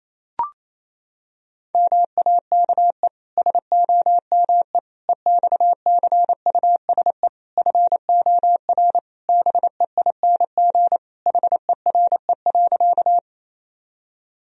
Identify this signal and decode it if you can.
Morse code, heard as a keyed tone, about 28 wpm, 700 Hz. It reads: MAKE SOME EXCUSE FOR BEING HERE.
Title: Sentences from Top 1000 Words - Review Part 3 - 28wpm